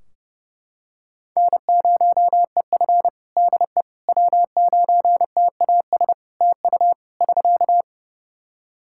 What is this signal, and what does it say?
Morse code, heard as a keyed tone, about 30 wpm, 700 Hz. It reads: N0EF DE W9TAS TU <SK>
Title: Practice QSO - Exchange 4 - Ending - 30wpm